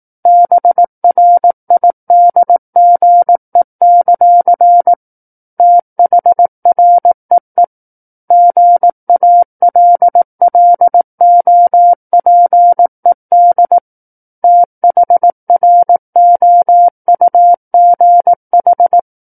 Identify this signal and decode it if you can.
Morse code, heard as a keyed tone, about 18 wpm, 700 Hz. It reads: BRIDGE; THREE GALLOPED THROUGH
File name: 18 WPM